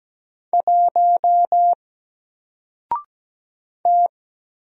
Morse code, heard as a keyed tone, about 17 wpm, 700 Hz. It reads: 1 T
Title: Single Letter-Number 17wpm